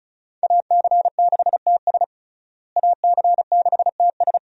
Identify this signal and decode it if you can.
Morse code, heard as a keyed tone, about 34 wpm, 700 Hz. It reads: AC6TS AC6TS
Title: USA Call Signs - 5 Characters - Repeated 2x - 35wpm